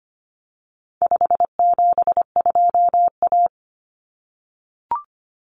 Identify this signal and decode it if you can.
Morse code, heard as a keyed tone, about 25 wpm, 700 Hz. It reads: 572A
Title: Common Cut Numbers - 4-Digit 25wpm